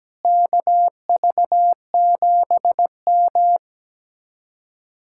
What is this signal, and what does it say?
Morse code, heard as a keyed tone, about 17 wpm, 700 Hz. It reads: KV7M
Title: Speed-Racing Common QSO Phrases - 17wpm